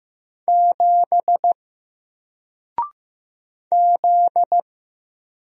Morse code, heard as a keyed tone, about 15 wpm, 700 Hz. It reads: 7 Z
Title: Single Character (Letter:Number:Punctuation:Prosign) Alt 15wpm